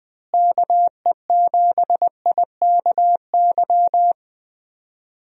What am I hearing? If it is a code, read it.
Morse code, heard as a keyed tone, about 21 wpm, 700 Hz. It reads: KE7IKY
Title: USA Call Signs - 6 Characters - Repeated 2x - 20wpm